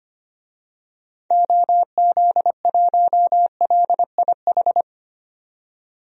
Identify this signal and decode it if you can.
Morse code, heard as a keyed tone, about 25 wpm, 700 Hz. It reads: OZ1LIH
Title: DXCC-100 Call Signs - 6 Characters - 25wpm